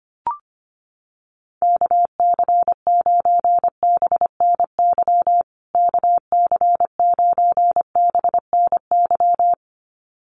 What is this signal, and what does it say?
Morse code, heard as a keyed tone, about 25 wpm, 700 Hz. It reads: KC9BNY KC9BNY
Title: USA Call Signs Repeated 2x 25wpm